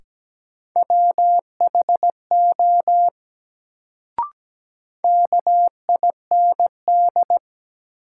Morse code, heard as a keyed tone, about 17 wpm, 700 Hz. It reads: WHO KIND
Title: Sets of 2 Words - New Word Drill - 17wpm